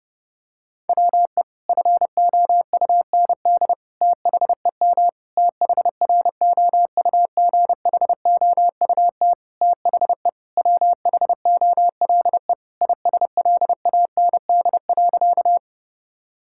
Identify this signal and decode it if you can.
Morse code, heard as a keyed tone, about 30 wpm, 700 Hz. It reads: WE FOUND THEM THROUGHOUT THE WHOLE ISLAND.
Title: Sentences from Top 1000 Words - Review Part 5 - 30wpm